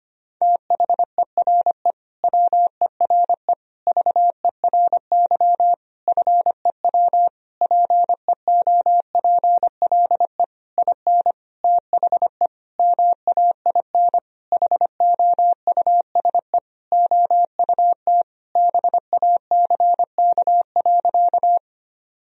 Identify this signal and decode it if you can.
Morse code, heard as a keyed tone, about 25 wpm, 700 Hz. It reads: THERE WERE VERY FEW PEOPLE IN THE MAIN HOUSE OUT BACK.
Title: Sentences from Top 1000 Words - Review Part 2 - 25wpm